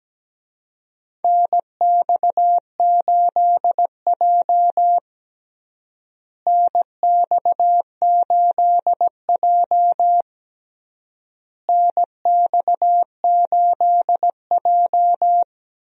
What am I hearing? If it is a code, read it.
Morse code, heard as a keyed tone, about 17 wpm, 700 Hz. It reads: NX8J NX8J NX8J
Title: USA Call Signs - 4 Characters - Repeated 3x - 17wpm